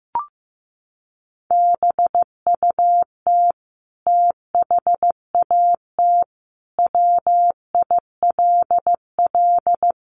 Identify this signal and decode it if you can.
Morse code, heard as a keyed tone, about 14 wpm, 700 Hz. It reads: BUT THAT WILL
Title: Sentences from Top 900 Words 15wpm